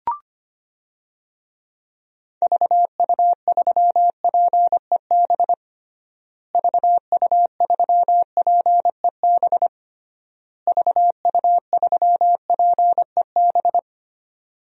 Morse code, heard as a keyed tone, about 25 wpm, 700 Hz. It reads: VU3PEB VU3PEB VU3PEB
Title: DXCC-100 Call Signs - 6 Characters - Repeated 3x - 25wpm